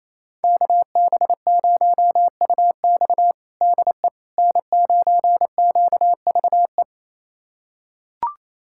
Morse code, heard as a keyed tone, about 28 wpm, 700 Hz. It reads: KB0UX DE N9QVE
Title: USA Callsign DE Callsign 28wpm